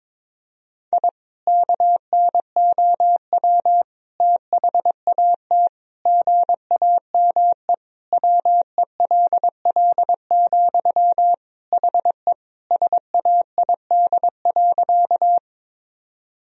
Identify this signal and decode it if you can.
Morse code, heard as a keyed tone, about 22 wpm, 700 Hz. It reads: I KNOW THAT GAME WELL, HE SAID.
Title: Sentences from Top 1000 Words - Review Part 3 - 22wpm